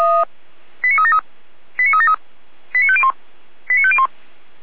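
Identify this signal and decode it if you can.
Five-tone selective call (ZVEI), heard as quick runs of five tones, tones of about 70 ms. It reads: 89373 89373 89501 89501
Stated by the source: ZVEI five-tone selective call received by SDR, from the Feuerwehr Orlamünde fire brigade audio server.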